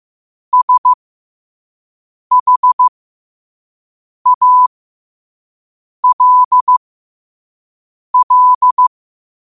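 Morse code, heard as a keyed tone, about 15 wpm, spaced as for 5 wpm, 1000 Hz. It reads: SHALL
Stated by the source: Morse code quote of the day at 5 wpm.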